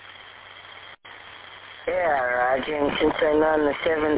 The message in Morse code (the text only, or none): none